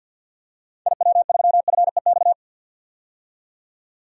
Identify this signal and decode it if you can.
Morse code, heard as a keyed tone, about 50 wpm, 700 Hz. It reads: IW3FEX
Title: DXCC-100 Call Signs - 6 Characters - 50wpm